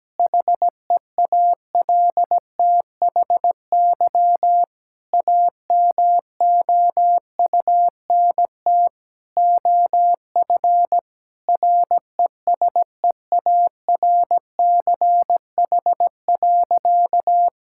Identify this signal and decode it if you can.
Morse code, heard as a keyed tone, about 17 wpm, 700 Hz. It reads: HEALTHY AMOUNT OF RESEARCH.
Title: Sentences from Top 1000 Words - Review Part 5 - 17wpm